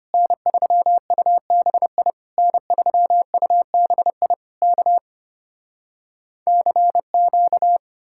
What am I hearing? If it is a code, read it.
Morse code, heard as a keyed tone, about 33 wpm, 700 Hz. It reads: N3UBI N3UBI K CQ